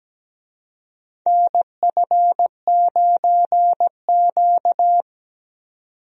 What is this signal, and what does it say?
Morse code, heard as a keyed tone, about 17 wpm, 700 Hz. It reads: NF9Q
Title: USA Call Signs - 4 Characters - 17wpm